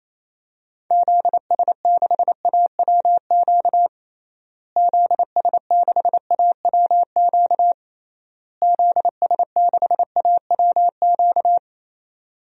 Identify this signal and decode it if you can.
Morse code, heard as a keyed tone, about 27 wpm, 700 Hz. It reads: ZS6AWQ ZS6AWQ ZS6AWQ
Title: DXCC-100 Call Signs - 6 Characters - Repeated 3x - 28wpm